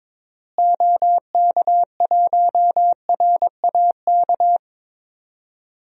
Morse code, heard as a keyed tone, about 22 wpm, 700 Hz. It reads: OK1RAK
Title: DXCC-100 Call Signs - 6 Characters - Repeated 3x - 22wpm